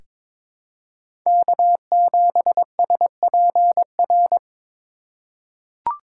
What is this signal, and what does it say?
Morse code, heard as a keyed tone, about 21 wpm, 700 Hz. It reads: K7SPR
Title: USA Call Signs - 5 Characters - 3x Character Spacing - 22wpm